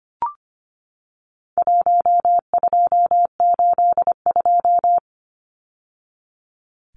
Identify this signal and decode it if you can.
Morse code, heard as a keyed tone, about 25 wpm, 700 Hz. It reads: 1282
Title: Numbers - 4-Digit 25wpm